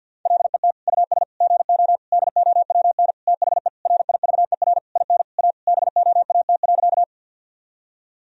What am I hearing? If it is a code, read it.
Morse code, heard as a keyed tone, about 50 wpm, 700 Hz. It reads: LET US GO DOWN THE RIVER IN A BOAT.
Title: Sentences from Top 1000 Words - Review Part 3 - 50wpm